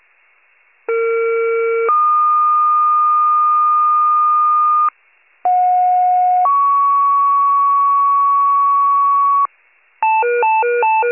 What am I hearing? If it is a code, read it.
Two-tone page, heard as a one-second tone, then a three-second tone, about 460 then 1220 Hz, about 730 then 1120 Hz. It is heard from 0.9 to 9.5 s.
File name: pager / 21 RESCUE / wcalert